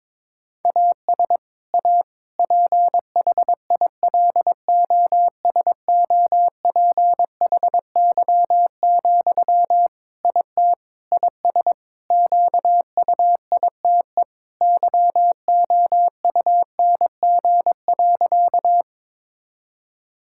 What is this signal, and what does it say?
Morse code, heard as a keyed tone, about 21 wpm, 700 Hz. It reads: AS A PHILOSOPHY, IT IS QUITE YOUNG.
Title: Sentences from Top 1000 Words - Review Part 6 - 22wpm